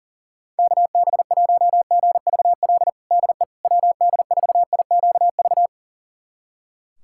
Morse code, heard as a keyed tone, about 40 wpm, 700 Hz. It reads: KB1GVL DE WD4IQV
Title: USA Callsign DE Callsign - 6 Characters - 40wpm